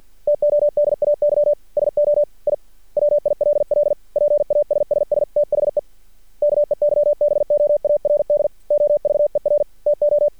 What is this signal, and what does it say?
Morse code, heard as a keyed tone, about 49 wpm, 590 Hz. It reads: TODAY SO I WILL PASSS THE KEYBOARD OVER TO
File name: archived / CRSnet